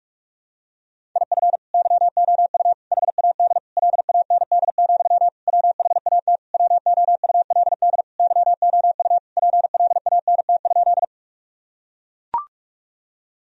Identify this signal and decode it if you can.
Morse code, heard as a keyed tone, about 44 wpm, 700 Hz. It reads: IF YOU HAD LAND, WHAT WOULD YOU PLANT?
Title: Sentences from Top 1000 Words - Review Part 1 - 45wpm